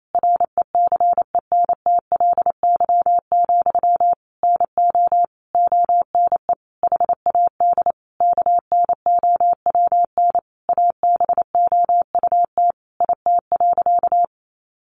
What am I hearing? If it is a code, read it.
Morse code, heard as a keyed tone, about 28 wpm, 700 Hz. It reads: RECENTLY, NO ONE HAD KNOWN ABOUT IT.
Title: Sentences from Top 1200 Words 28wpm